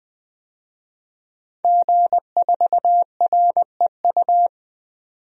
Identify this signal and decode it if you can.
Morse code, heard as a keyed tone, about 21 wpm, 700 Hz. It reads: G4REU
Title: DXCC-100 Call Signs - 5 Characters - 20wpm